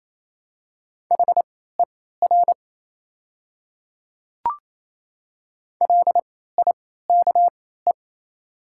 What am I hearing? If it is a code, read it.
Morse code, heard as a keyed tone, about 28 wpm, spaced as for 16 wpm, 700 Hz. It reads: HER LIKE
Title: 3x Character Spacing - Top 100 Words - 28wpm